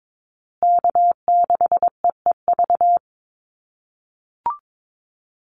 Morse code, heard as a keyed tone, about 22 wpm, 700 Hz. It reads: K6EEV
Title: USA Call Signs Repeated 2x 22wpm